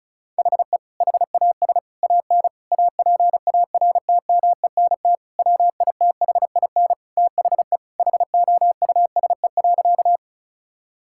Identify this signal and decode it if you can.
Morse code, heard as a keyed tone, about 36 wpm, 700 Hz. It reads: HE HAS AN APARTMENT WITHIN THE HOUSE.
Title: Sentences from Top 1000 Words - Review Part 6 - 35wpm